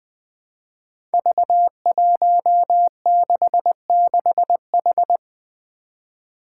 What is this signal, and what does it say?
Morse code, heard as a keyed tone, about 20 wpm, 700 Hz. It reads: V166H
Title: Five Challenging Letter-Number Combinations 20wpm